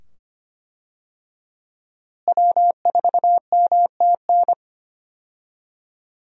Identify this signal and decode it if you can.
Morse code, heard as a keyed tone, about 25 wpm, 700 Hz. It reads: W4MTN
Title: USA Call Signs - 5 Characters - Repeated 2x - 25wpm